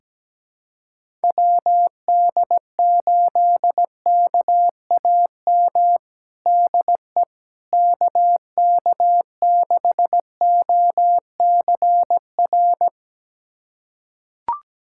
Morse code, heard as a keyed tone, about 17 wpm, 700 Hz. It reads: WD8KAM DE KK6OCR
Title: USA Callsign DE Callsign 17wpm